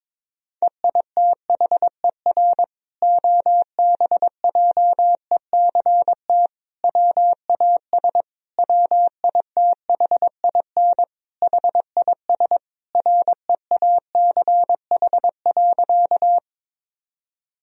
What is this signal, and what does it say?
Morse code, heard as a keyed tone, about 21 wpm, 700 Hz. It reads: EITHER OBJECT WAS WITHIN HIS REACH.
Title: Sentences from Top 1000 Words - Review Part 4 - 22wpm